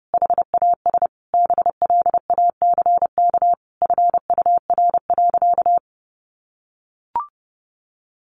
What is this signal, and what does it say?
Morse code, heard as a keyed tone, about 30 wpm, 700 Hz. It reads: HAS BLACK FUR.
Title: Easy Phrases 30wpm